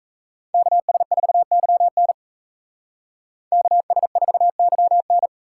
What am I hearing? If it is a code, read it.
Morse code, heard as a keyed tone, about 33 wpm, 700 Hz. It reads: KS4YN KS4YN
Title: Speed Racing - USA 2x2 Call Signs 28wpm